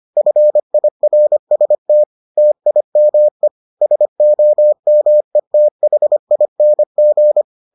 Morse code, heard as a keyed tone, about 25 wpm, 600 Hz. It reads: FIRST TIME SOMETHING